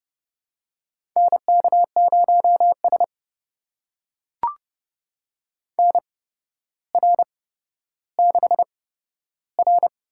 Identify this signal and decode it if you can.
Morse code, heard as a keyed tone, about 30 wpm, spaced as for 13 wpm, 700 Hz. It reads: NK0S NR6R
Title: USA Call Signs - 4 Characters - 8x Character Spacing - 30wpm